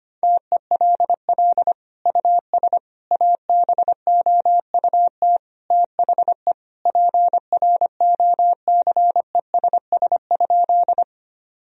Morse code, heard as a keyed tone, about 25 wpm, 700 Hz. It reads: TELL US ABOUT THE PROCESS?
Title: Sentences from Top 1000 Words - Review Part 3 - 25wpm